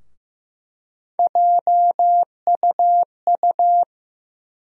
Morse code, heard as a keyed tone, about 15 wpm, 700 Hz. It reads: JUU
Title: USA Call Sign Postfixes - 3 Letters 15wpm